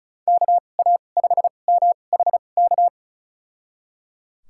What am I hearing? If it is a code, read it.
Morse code, heard as a keyed tone, about 36 wpm, spaced as for 25 wpm, 700 Hz. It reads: KA5MHK
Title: USA Call Signs - 6 Characters - 2x Character Spacing - 35wpm